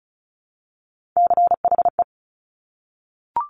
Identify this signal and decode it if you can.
Morse code, heard as a keyed tone, about 36 wpm, 700 Hz. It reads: CHE
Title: Top Three Letter Combinations in Words 35wpm